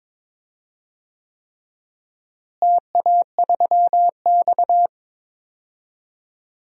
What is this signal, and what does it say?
Morse code, heard as a keyed tone, about 22 wpm, 700 Hz. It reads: TA3X
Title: DXCC-100 Call Signs - 4 Characters - Repeated 3x - 22wpm